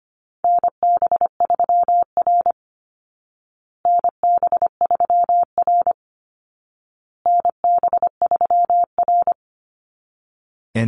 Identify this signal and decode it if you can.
Morse code, heard as a keyed tone, about 25 wpm, 700 Hz. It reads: NB3R NB3R NB3R
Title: Call Signs Repeated 2x - 25wpm